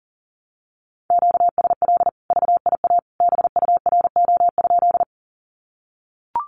Morse code, heard as a keyed tone, about 40 wpm, 700 Hz. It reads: QSL VIA BURO?
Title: Common QSO Phrases 40wpm